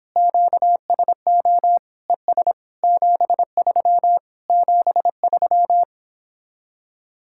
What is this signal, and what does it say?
Morse code, heard as a keyed tone, about 26 wpm, 700 Hz. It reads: QSO ES 73 73